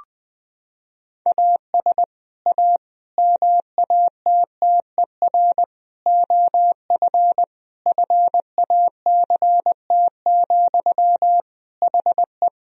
Morse code, heard as a keyed tone, about 21 wpm, 700 Hz. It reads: AS A MATTER OF FACT, HE
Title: Sentences from Top 1000 Words - Review Part 3 - 20wpm